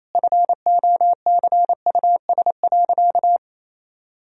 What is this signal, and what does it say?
Morse code, heard as a keyed tone, about 28 wpm, 700 Hz. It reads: FOCUS.
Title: Sentences from Top 1000 Words - Review Part 4 - 28wpm